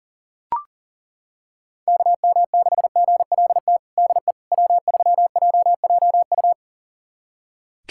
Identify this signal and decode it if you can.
Morse code, heard as a keyed tone, about 40 wpm, 700 Hz. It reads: KM6GLT DE W3JJU
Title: USA Callsign DE Callsign 40wpm